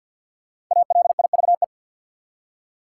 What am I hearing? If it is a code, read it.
Morse code, heard as a keyed tone, about 50 wpm, 700 Hz. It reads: ALIVE
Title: Five Letter Words - 50wpm